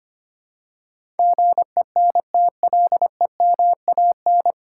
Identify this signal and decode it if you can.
Morse code, heard as a keyed tone, about 25 wpm, 700 Hz. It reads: GENTLEMAN
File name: Nine Letter Words - 25wpm